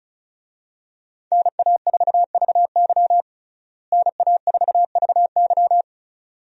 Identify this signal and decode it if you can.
Morse code, heard as a keyed tone, about 36 wpm, 700 Hz. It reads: NA4VY NA4VY
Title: USA Call Signs - 5 Characters - Repeated 2x - 35wpm